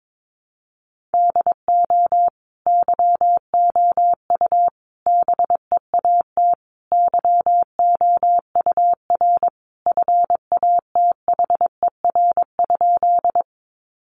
Morse code, heard as a keyed tone, about 22 wpm, 700 Hz. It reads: DO YOU BEAT YOUR FATHER?
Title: Sentences from Top 1300 Words 22wpm